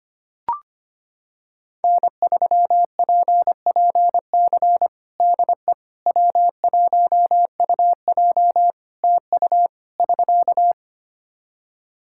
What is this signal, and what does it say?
Morse code, heard as a keyed tone, about 25 wpm, 700 Hz. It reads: N3PPC DE W1UJ TU <SK>
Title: Practice QSO - Exchange 4 - Ending - 25wpm